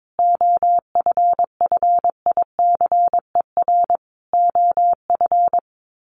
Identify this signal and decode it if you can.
Morse code, heard as a keyed tone, about 22 wpm, 700 Hz. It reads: OFFICER OF
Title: Sets of 5 Words Encore2 22wpm